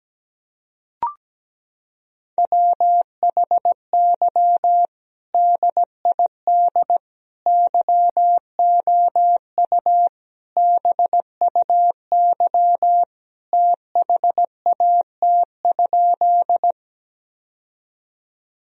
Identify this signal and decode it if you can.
Morse code, heard as a keyed tone, about 17 wpm, 700 Hz. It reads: WHY DID YOU BUY THAT?
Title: Sentences from Top 1000 Words - Review Part 4 - 17wpm